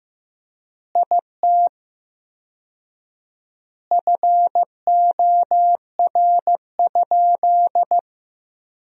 Morse code, heard as a keyed tone, about 15 wpm, 700 Hz. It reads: IT FOR?
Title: 4x Word-Spacing - Sentences from Top 100 Words - 15wpm